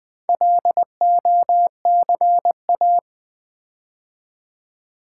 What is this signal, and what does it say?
Morse code, heard as a keyed tone, about 20 wpm, 700 Hz. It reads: LOCA
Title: Build A Word - 4 Letters At A Time - 8-Letter Words - 20wpm